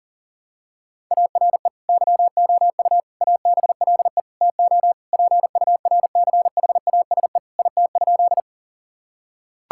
Morse code, heard as a keyed tone, about 40 wpm, 700 Hz. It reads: ARE YOU ABLE TO PURCHASE IT?
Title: Sentences from Top 1000 Words - Review Part 5 - 40wpm